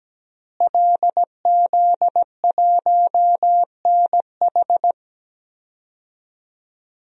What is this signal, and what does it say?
Morse code, heard as a keyed tone, about 17 wpm, 700 Hz. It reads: LZ1NH
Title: DXCC-100 Call Signs - 5 Characters - Repeated 2x - 17wpm